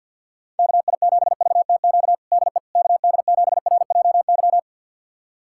Morse code, heard as a keyed tone, about 52 wpm, 700 Hz. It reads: KI7VTQ DE KD7RJY
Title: USA Callsign DE Callsign - 6 Characters - 50wpm